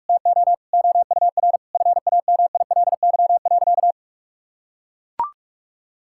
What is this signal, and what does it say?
Morse code, heard as a keyed tone, about 41 wpm, 700 Hz. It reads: TO OUR FAMILY.
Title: Sentences from Top 1000 Words - Review Part 6 - 45wpm